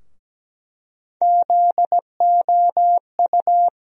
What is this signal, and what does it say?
Morse code, heard as a keyed tone, about 17 wpm, 700 Hz. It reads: ZOU
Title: USA Call Sign Postfixes - 3 Letters 17wpm